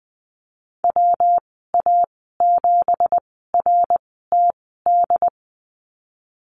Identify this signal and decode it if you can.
Morse code, heard as a keyed tone, about 20 wpm, spaced as for 14 wpm, 700 Hz. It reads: WA7RTD
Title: 167 Lesson 27g - Number 7 - Call Signs 20wpm